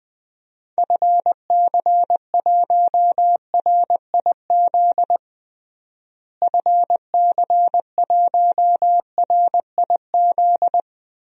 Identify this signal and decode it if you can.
Morse code, heard as a keyed tone, about 20 wpm, 700 Hz. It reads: FC1RIZ FC1RIZ
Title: DXCC-100 Call Signs - 6 Characters - Repeated 2x - 20wpm